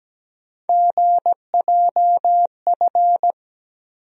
Morse code, heard as a keyed tone, about 17 wpm, 700 Hz. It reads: GJF
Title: USA Call Sign Postfixes - 3 Letters 17wpm